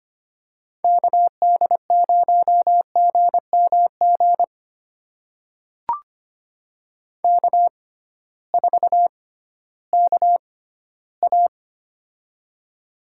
Morse code, heard as a keyed tone, about 25 wpm, spaced as for 14 wpm, 700 Hz. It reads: KD0GMG K4KA
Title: USA Call Signs - 6x Character Spacing - 25wpm